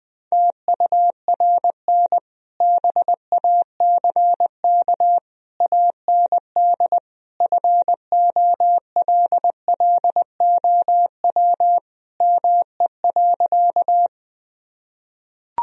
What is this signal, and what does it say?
Morse code, heard as a keyed tone, about 20 wpm, 700 Hz. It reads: TURN BACK AND FOLLOW ME.
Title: Sentences from Top 1000 Words - Review Part 1 - 20wpm